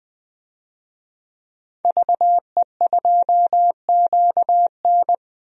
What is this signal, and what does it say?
Morse code, heard as a keyed tone, about 20 wpm, 700 Hz. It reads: VE2QN
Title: DXCC-100 Call Signs - 5 Characters - 20wpm